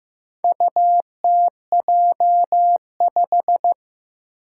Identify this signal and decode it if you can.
Morse code, heard as a keyed tone, about 15 wpm, 700 Hz. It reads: UTJ5
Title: Four Challenging Letter-Number Combinations 15wpm